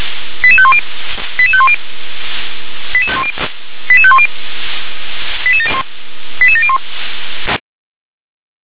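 Five-tone selective call (ZVEI), heard as quick runs of five tones, tones of about 70 ms. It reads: 88510 88510 88510 88701
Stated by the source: ZVEI five-tone selective call received on an SDR.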